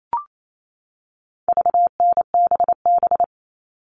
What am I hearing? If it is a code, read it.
Morse code, heard as a keyed tone, about 28 wpm, 700 Hz. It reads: VNBB
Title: All Cut Numbers - 4-Digit 28wpm